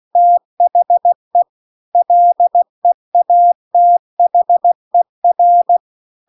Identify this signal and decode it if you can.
Morse code, heard as a keyed tone, about 16 wpm, 700 Hz. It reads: THE LEATHER
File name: CW-The-Big-Twoheaded-River-2-16WPM